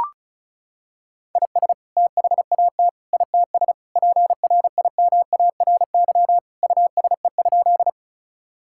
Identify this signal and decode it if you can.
Morse code, heard as a keyed tone, about 34 wpm, 700 Hz. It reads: IS THAT ITS PRIMARY USE?
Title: Sentences from Top 1000 Words - Review Part 5 - 35wpm